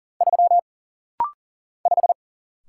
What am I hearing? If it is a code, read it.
Morse code, heard as a keyed tone, about 40 wpm, 700 Hz. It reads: 3 5
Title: Single Letter-Number - Rapid-Fire 40wpm